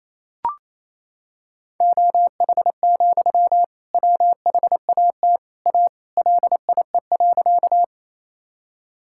Morse code, heard as a keyed tone, about 28 wpm, 700 Hz. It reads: OH, WHAT A LIE.
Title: Sentences from Top 1600 Words 28wpm